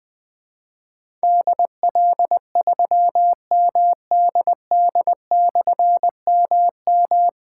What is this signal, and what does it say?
Morse code, heard as a keyed tone, about 20 wpm, 700 Hz. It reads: DL3MDD/MM
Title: Difficult Call Signs 20wpm